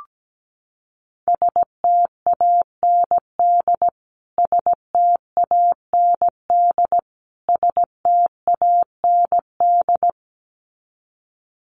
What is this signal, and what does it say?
Morse code, heard as a keyed tone, about 17 wpm, 700 Hz. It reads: STAND STAND STAND
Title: Top 500 Words Repeated 17wpm